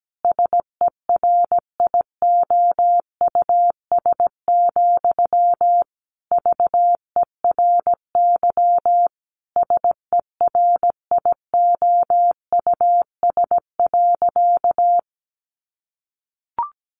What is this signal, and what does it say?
Morse code, heard as a keyed tone, about 17 wpm, 700 Hz. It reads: SERIOUS, VERY SERIOUS.
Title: Sentences from Top 1100 Words 17wpm